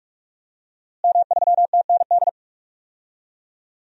Morse code, heard as a keyed tone, about 43 wpm, 700 Hz. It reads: M3TND
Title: DXCC-100 Call Signs - 5 Characters - Repeated 2x - 45wpm